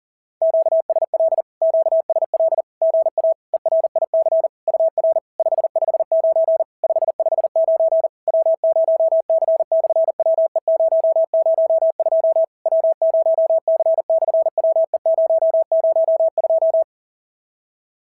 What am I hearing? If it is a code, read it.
Morse code, heard as a keyed tone, about 40 wpm, 650 Hz. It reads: QSL QSL GA ERIC UR 559 559 W0C/WE002 W0C/WE002
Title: SOTA QSO Practice 40wpm